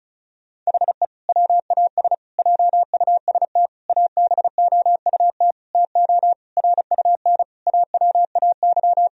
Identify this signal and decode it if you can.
Morse code, heard as a keyed tone, about 34 wpm, 700 Hz. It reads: HE WAS JUST ABOUT TO RUN AWAY
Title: Sentences from Top 1000 Words - Review Part 2 - 35wpm